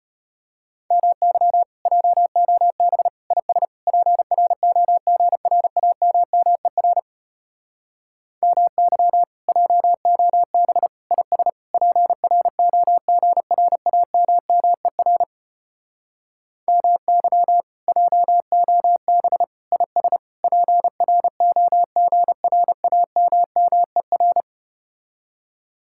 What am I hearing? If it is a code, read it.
Morse code, heard as a keyed tone, about 33 wpm, 700 Hz. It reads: MY JOB IS PROGRAMMER MY JOB IS PROGRAMMER MY JOB IS PROGRAMMER